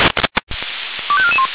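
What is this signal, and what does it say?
Five-tone selective call (ZVEI), heard as quick runs of five tones, tones of about 70 ms. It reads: 25441
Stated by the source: Five-tone ZVEI selective call received on an SDR, recorded by Feuerwehr Orlamünde.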